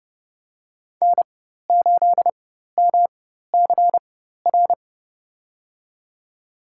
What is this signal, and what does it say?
Morse code, heard as a keyed tone, about 32 wpm, spaced as for 14 wpm, 700 Hz. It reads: N8MCR
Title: USA Call Signs - 5 Characters - 4x Character Spacing - 30wpm